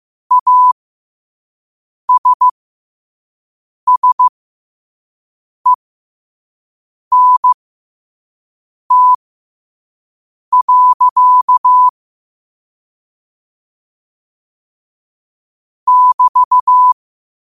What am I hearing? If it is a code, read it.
Morse code, heard as a keyed tone, about 15 wpm, spaced as for 5 wpm, 1000 Hz. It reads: ASSENT. <BT>
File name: qotdfiles / 5wpm